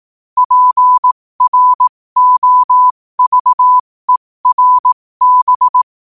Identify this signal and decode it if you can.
Morse code, heard as a keyed tone, about 18 wpm, spaced as for 15 wpm, 1000 Hz. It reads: PROVERB
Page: New quotes every day in morse code at 15 Words per minute.